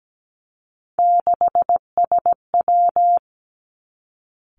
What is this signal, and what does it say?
Morse code, heard as a keyed tone, about 17 wpm, 700 Hz. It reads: 6SW
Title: Three Challenging Letter-Number Combinations 17wpm